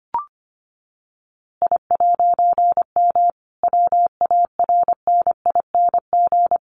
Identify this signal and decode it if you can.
Morse code, heard as a keyed tone, about 25 wpm, 700 Hz. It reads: I'M WARNING
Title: A Sound Of Thunder 25wpm